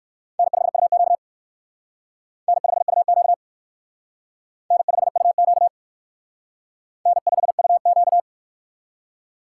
Morse code, heard as a keyed tone, about 56 wpm, 700 Hz. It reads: N5UQ N5UQ N5UQ N5UQ
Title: Speed Racing - USA 1x2 Call Signs 45wpm